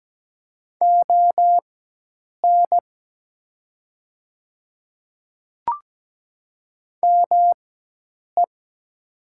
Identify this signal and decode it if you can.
Morse code, heard as a keyed tone, about 17 wpm, spaced as for 8 wpm, 700 Hz. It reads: ON ME
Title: 4x Character Spacing - Top 200 Words - 17wpm